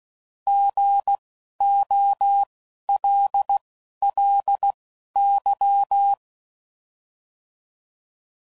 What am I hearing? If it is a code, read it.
Morse code, heard as a keyed tone, about 16 wpm, spaced as for 11 wpm, 800 Hz. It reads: GOLLY
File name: Prefixes / By First Letter